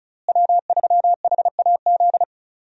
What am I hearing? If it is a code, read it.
Morse code, heard as a keyed tone, about 36 wpm, 700 Hz. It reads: W3HAZ
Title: USA Call Signs - 5 Characters - Repeated 3x - 35wpm